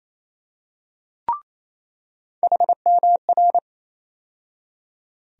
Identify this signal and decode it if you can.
Morse code, heard as a keyed tone, about 28 wpm, 700 Hz. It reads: HMR
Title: USA Call Sign Postfixes - 3 Letters 28wpm